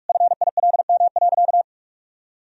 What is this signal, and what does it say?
Morse code, heard as a keyed tone, about 46 wpm, 700 Hz. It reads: FILM.
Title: Sentences from Top 1000 Words - Review Part 3 - 45wpm